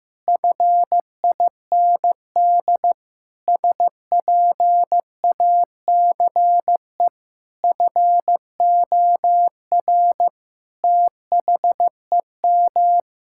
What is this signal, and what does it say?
Morse code, heard as a keyed tone, about 15 wpm, 700 Hz. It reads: FIND SPACE FOR THEM
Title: Sentences from Top 1000 Words - Review Part 5 - 15wpm